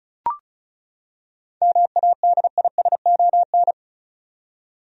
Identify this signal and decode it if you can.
Morse code, heard as a keyed tone, about 34 wpm, 700 Hz. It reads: MADISON
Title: USA Common Names - 35wpm